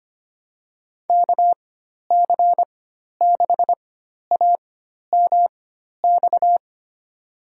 Morse code, heard as a keyed tone, about 25 wpm, spaced as for 12 wpm, 700 Hz. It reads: KC6AMX
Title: USA Call Signs - 4x Character Spacing - 25wpm